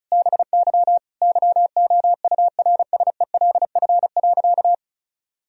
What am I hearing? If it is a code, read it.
Morse code, heard as a keyed tone, about 36 wpm, 700 Hz. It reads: BY YOURSELF.
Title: Sentences from Top 1000 Words - Review Part 4 - 35wpm